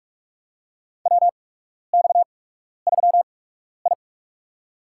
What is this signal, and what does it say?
Morse code, heard as a keyed tone, about 47 wpm, spaced as for 12 wpm, 700 Hz. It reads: WX3I
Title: USA Call Signs - 4 Characters - 8x Character Spacing - 45wpm